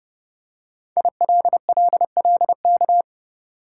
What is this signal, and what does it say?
Morse code, heard as a keyed tone, about 30 wpm, 700 Hz. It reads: ILLLK
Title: Code Groups - 5 Letters - 30wpm